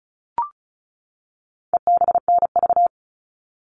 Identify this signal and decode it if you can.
Morse code, heard as a keyed tone, about 34 wpm, 700 Hz. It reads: EBNV
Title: All Cut Numbers - 4-Digit 35wpm